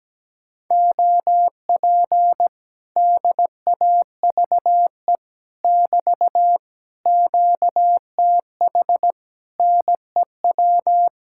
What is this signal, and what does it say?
Morse code, heard as a keyed tone, about 17 wpm, 700 Hz. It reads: OP DAVE <BT> QTH NEW
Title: Speed-Racing Common QSO Phrases - 17wpm